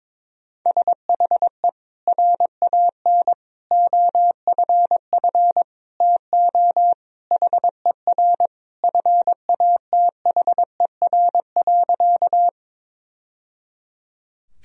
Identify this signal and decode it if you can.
Morse code, heard as a keyed tone, about 21 wpm, 700 Hz. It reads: SHE RAN OFF TO HER FATHER.
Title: Sentences from Top 1000 Words - Review Part 3 - 22wpm